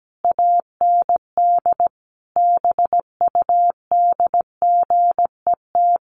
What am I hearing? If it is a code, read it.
Morse code, heard as a keyed tone, about 17 wpm, 700 Hz. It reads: AND BUDGET
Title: Sets of 5 Words Encore2 17wpm